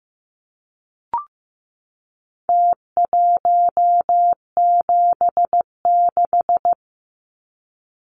Morse code, heard as a keyed tone, about 15 wpm, 700 Hz. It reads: T176
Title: Numbers - Contest Style - 4-Digit 15wpm